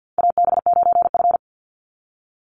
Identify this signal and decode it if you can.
Morse code, heard as a keyed tone, about 50 wpm, 700 Hz. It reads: AB9F
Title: USA Call Signs 50wpm